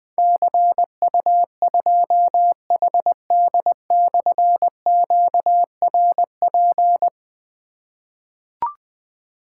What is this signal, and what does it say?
Morse code, heard as a keyed tone, about 21 wpm, 700 Hz. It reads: CU2HD/QRP
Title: Difficult Call Signs 20wpm